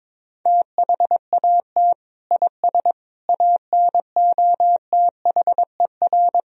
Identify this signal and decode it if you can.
Morse code, heard as a keyed tone, about 22 wpm, 700 Hz. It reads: THAT IS ANOTHER
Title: Sentences from Top 1000 Words - Review Part 1 - 22wpm